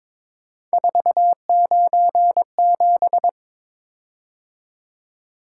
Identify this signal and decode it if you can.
Morse code, heard as a keyed tone, about 22 wpm, 700 Hz. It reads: 497
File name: practice / 22wpm / Three Letter-Number Combinations - 22wpm